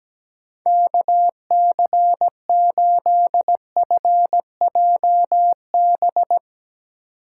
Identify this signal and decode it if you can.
Morse code, heard as a keyed tone, about 18 wpm, 700 Hz. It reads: KC8FJB
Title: Speed Racing - USA 2x3 Call Signs 17wpm